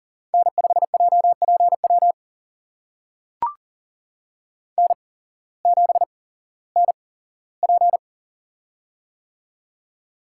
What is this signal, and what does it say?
Morse code, heard as a keyed tone, about 40 wpm, spaced as for 19 wpm, 700 Hz. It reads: N5JPW N7NP
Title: USA Call Signs - 8x Character Spacing - 40wpm